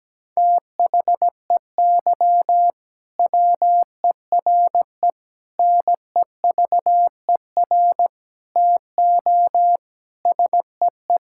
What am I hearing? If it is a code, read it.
Morse code, heard as a keyed tone, about 17 wpm, 700 Hz. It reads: THEY WERE NEVER TO SEE
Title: Sentences from Top 1000 Words - Review Part 6 - 17wpm